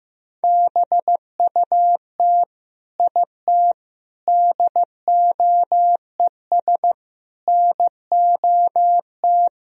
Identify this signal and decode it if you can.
Morse code, heard as a keyed tone, about 16 wpm, 700 Hz. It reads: BUT IT DOES NOT
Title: Sentences from Top 1000 Words - Review Part 2 - 15wpm